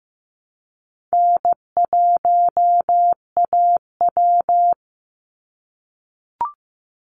Morse code, heard as a keyed tone, about 15 wpm, 700 Hz. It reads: N1AW
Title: NA SOTA Chaser Callsigns 15wpm